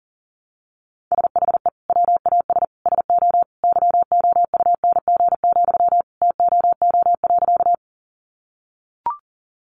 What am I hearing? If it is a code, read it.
Morse code, heard as a keyed tone, about 40 wpm, 700 Hz. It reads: SHE WAS SO YOUNG, TOO.
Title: Sentences from Top 400 Words 40wpm